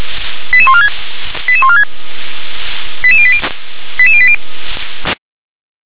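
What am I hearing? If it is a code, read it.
Five-tone selective call (ZVEI), heard as quick runs of five tones, tones of about 70 ms. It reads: 88146 88146 88080 88080
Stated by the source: Five-tone ZVEI selective call received on an SDR.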